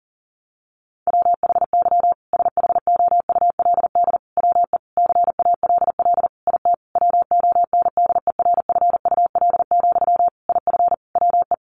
Morse code, heard as a keyed tone, about 40 wpm, 700 Hz. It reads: WHY SHOULD WE CALL IT WONDERFUL, IF WE
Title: Sentences from Top 1100 Words 40wpm